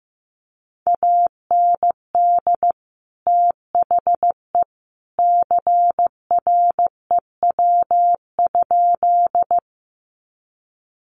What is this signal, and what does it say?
Morse code, heard as a keyed tone, about 15 wpm, 700 Hz. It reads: AND THE CREW?
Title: Sentences from Top 1500 Words 15wpm